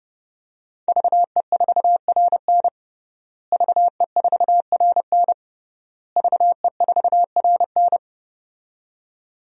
Morse code, heard as a keyed tone, about 30 wpm, 700 Hz. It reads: VE4RN VE4RN VE4RN
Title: DXCC-100 Call Signs - 5 Characters - Repeated 3x - 30wpm